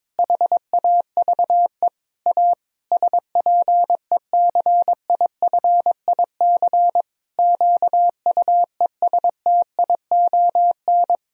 Morse code, heard as a keyed tone, about 22 wpm, 700 Hz. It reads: HAVE A SPECIFIC QUESTION
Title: Sentences from Top 1000 Words - Review Part 3 - 22wpm